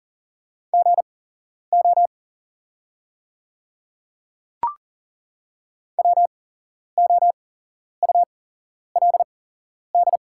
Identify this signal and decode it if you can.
Morse code, heard as a keyed tone, about 39 wpm, spaced as for 11 wpm, 700 Hz. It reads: GO WOULD
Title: 8x Character Spacing - Top 100 Words - 40wpm